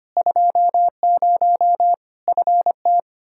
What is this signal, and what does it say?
Morse code, heard as a keyed tone, about 25 wpm, 700 Hz. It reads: 20 FT
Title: Practice QSO - Exchange 2 - Info - 25wpm